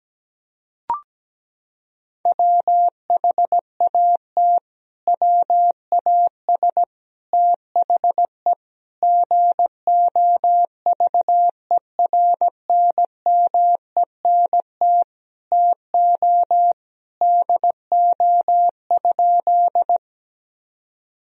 Morse code, heard as a keyed tone, about 17 wpm, 700 Hz. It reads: WHAT WAS THE GOVERNMENT TO DO?
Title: Sentences from Top 1000 Words - Review Part 3 - 17wpm